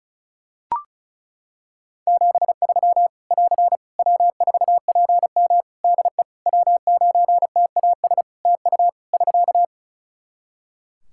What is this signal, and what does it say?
Morse code, heard as a keyed tone, about 36 wpm, 700 Hz. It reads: 73 <AR> W4PM DE W9TAS TU <SK>
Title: K1USN SST QSO Practice 35wpm